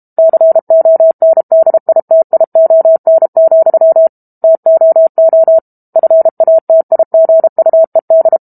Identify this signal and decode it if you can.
Morse code, heard as a keyed tone, about 32 wpm, 650 Hz. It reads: CONDITION, TOO FATIGUED
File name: Book Two / 06-Chapter 6 - 32 WPM